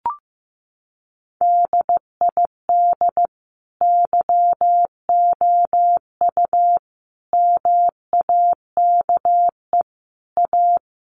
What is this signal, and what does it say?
Morse code, heard as a keyed tone, about 15 wpm, 700 Hz. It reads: DID YOU MAKE A
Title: Sentences from Top 1300 Words 15wpm